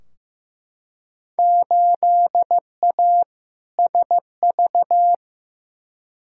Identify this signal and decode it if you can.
Morse code, heard as a keyed tone, about 15 wpm, 700 Hz. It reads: 8A SV
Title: Field Day Exchanges Encore - 15wpm